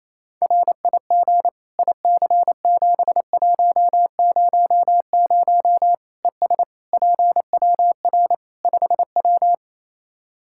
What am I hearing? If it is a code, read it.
Morse code, heard as a keyed tone, about 28 wpm, 700 Hz. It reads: RIG IC7100 ES PWR 5W
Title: Practice QSO - Exchange 2 - Info - 28wpm